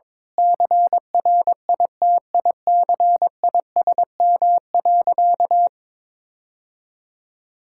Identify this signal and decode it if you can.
Morse code, heard as a keyed tone, about 22 wpm, 700 Hz. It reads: CRITICISM.
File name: practice / 22wpm / Sentences with 8-Word Phrases - 22wpm